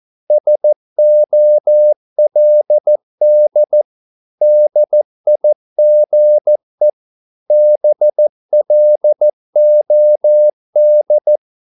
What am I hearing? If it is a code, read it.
Morse code, heard as a keyed tone, about 14 wpm, 600 Hz. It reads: SOLD DIGE BLOD
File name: Korte ord DK 14wpm